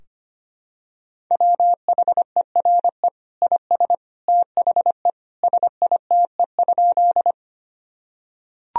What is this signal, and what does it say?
Morse code, heard as a keyed tone, about 25 wpm, 700 Hz. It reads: WHERE IS THE SITE?
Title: Sentences from Top 1000 Words - Review Part 4 - 25wpm